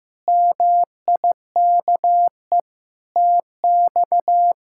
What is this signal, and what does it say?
Morse code, heard as a keyed tone, about 15 wpm, 700 Hz. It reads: MIKE TX
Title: K1USN SST QSO Practice - 15wpm